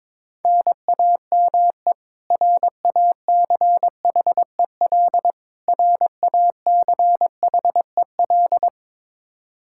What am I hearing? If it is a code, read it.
Morse code, heard as a keyed tone, about 22 wpm, 700 Hz. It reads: NAME RACHEL RACHEL
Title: Practice QSO - Exchange 1 - Beginning - 22wpm